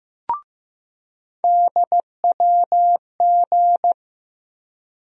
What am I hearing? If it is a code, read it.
Morse code, heard as a keyed tone, about 15 wpm, 700 Hz. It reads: DWG
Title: USA Call Sign Postfixes - 3 Letters 15wpm